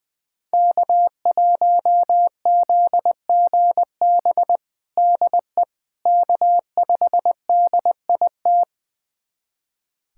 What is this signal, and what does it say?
Morse code, heard as a keyed tone, about 21 wpm, 700 Hz. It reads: K1ZGB DE K5DIT
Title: USA Callsign DE Callsign - 5 Characters - 20wpm